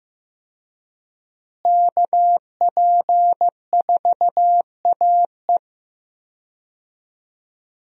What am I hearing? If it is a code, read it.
Morse code, heard as a keyed tone, about 15 wpm, 700 Hz. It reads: KP4AE
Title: USA Call Signs - 5 Characters - Repeated 2x - 15wpm